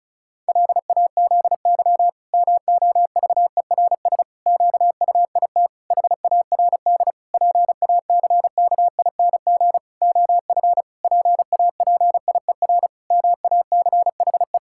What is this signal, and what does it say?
Morse code, heard as a keyed tone, about 36 wpm, 700 Hz. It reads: LAZY MOVERS QUIT HARD PACKING OF PAPIER MACHE
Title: Pangram Challenge v2 - 35wpm